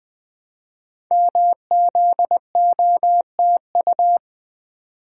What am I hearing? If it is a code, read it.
Morse code, heard as a keyed tone, about 20 wpm, 700 Hz. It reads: MZOTU
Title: Code Groups - 5 Letters - 20wpm